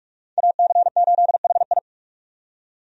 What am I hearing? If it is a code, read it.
Morse code, heard as a keyed tone, about 42 wpm, 700 Hz. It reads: AC8HI
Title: USA Call Signs - 5 Characters - 45wpm